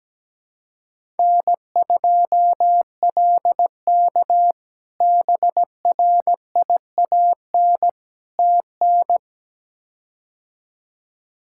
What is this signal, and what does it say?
Morse code, heard as a keyed tone, about 17 wpm, 700 Hz. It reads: N2LK BRIAN TN
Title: K1USN SST QSO Practice - 17wpm